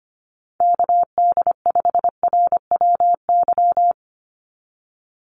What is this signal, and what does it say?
Morse code, heard as a keyed tone, about 25 wpm, 700 Hz. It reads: KD5RWY
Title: USA Call Signs 25wpm